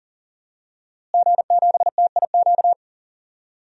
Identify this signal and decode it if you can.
Morse code, heard as a keyed tone, about 41 wpm, 700 Hz. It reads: G7TIQ
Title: DXCC-100 Call Signs - 5 Characters - Repeated 3x - 40wpm